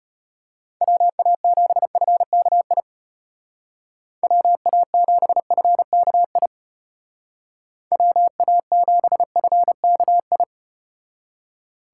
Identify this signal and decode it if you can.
Morse code, heard as a keyed tone, about 33 wpm, 700 Hz. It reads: WA7FKI WA7FKI WA7FKI